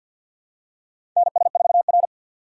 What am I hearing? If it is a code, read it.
Morse code, heard as a keyed tone, about 52 wpm, 700 Hz. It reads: NS4R
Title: USA Call Signs - 4x Character Spacing - 50wpm